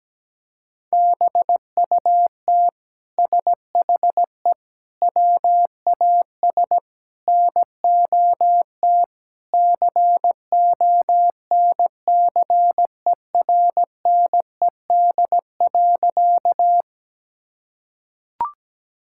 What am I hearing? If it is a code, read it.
Morse code, heard as a keyed tone, about 17 wpm, 700 Hz. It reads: BUT SHE WAS NOT CONCERNED.
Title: Sentences from Top 1000 Words - Review Part 6 - 17wpm